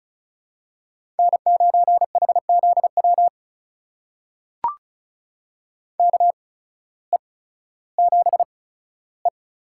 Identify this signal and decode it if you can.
Morse code, heard as a keyed tone, about 36 wpm, spaced as for 16 wpm, 700 Hz. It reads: N9HZW KE7E
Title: USA Call Signs - 5 Characters - 8x Character Spacing - 35wpm